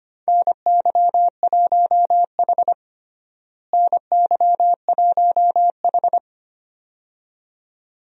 Morse code, heard as a keyed tone, about 25 wpm, 700 Hz. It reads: NY1H NY1H
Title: USA Call Signs - 4 Characters - Repeated 2x - 25wpm